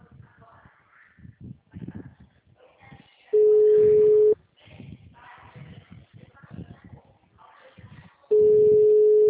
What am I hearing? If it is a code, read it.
Ringback tone (Europe, 425 Hz).